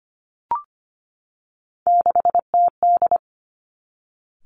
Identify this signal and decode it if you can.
Morse code, heard as a keyed tone, about 25 wpm, 700 Hz. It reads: 6TD
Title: Three Challenging Letter-Number Combinations 25wpm